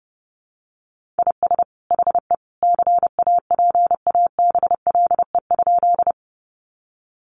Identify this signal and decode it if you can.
Morse code, heard as a keyed tone, about 30 wpm, 700 Hz. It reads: IS HE CAPABLE?
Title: Sentences from Top 1200 Words 30wpm